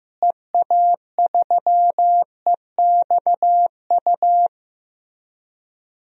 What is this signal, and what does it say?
Morse code, heard as a keyed tone, about 15 wpm, 700 Hz. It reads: EA3EXU
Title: DXCC-100 Call Signs - 6 Characters - Repeated 2x - 15wpm